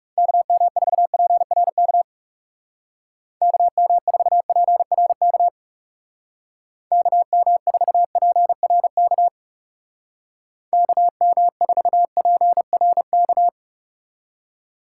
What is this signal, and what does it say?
Morse code, heard as a keyed tone, about 35 wpm, 700 Hz. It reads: KM4PRK KM4PRK KM4PRK KM4PRK
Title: Speed Racing - USA Call Signs 30wpm